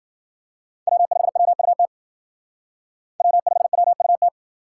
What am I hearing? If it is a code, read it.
Morse code, heard as a keyed tone, about 51 wpm, 700 Hz. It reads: W5WUT W5WUT